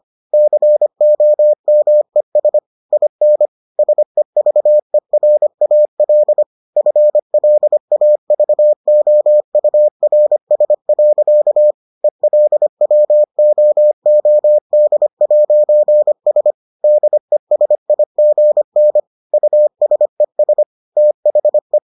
Morse code, heard as a keyed tone, about 25 wpm, 600 Hz. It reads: COMES IN SEVERAL FLAVOURS. ELWOOD'S DESIGN USES THE